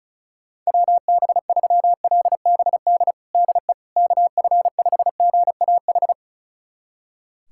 Morse code, heard as a keyed tone, about 34 wpm, 700 Hz. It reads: WB3LBD DE KF5GAH
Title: USA Callsign DE Callsign 35wpm